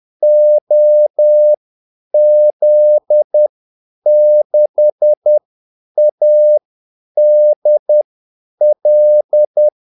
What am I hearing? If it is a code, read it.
Morse code, heard as a keyed tone, about 10 wpm, spaced as for 8 wpm, 600 Hz. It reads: OZ6ADL
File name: Callsigns-10-8wpm